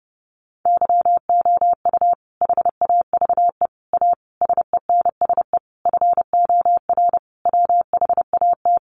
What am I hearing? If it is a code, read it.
Morse code, heard as a keyed tone, about 30 wpm, 700 Hz. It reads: YOU HAVE A SENSE FOR WHAT
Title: Sentences from Top 700 Words 30wpm